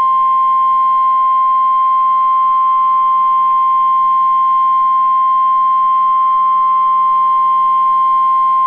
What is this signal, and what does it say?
Emergency Alert System (SAME) tones (the 1050 Hz attention tone). It is heard from 1.1 to 8.1 s.